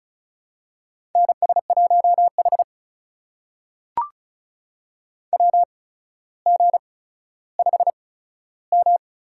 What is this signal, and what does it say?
Morse code, heard as a keyed tone, about 36 wpm, spaced as for 15 wpm, 700 Hz. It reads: NS1H WG5M
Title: USA Call Signs - 4 Characters - 8x Character Spacing - 35wpm